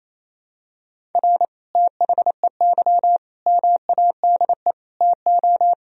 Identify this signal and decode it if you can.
Morse code, heard as a keyed tone, about 28 wpm, 700 Hz. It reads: R THEY MADE TO
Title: Sentences from Top 400 Words and Abbreviations 28wpm